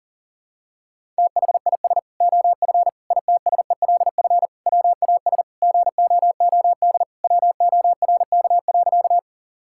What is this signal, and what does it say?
Morse code, heard as a keyed tone, about 40 wpm, 700 Hz. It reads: THIS OF ITSELF WAS GOOD WORK.
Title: Sentences from Top 1000 Words - Review Part 5 - 40wpm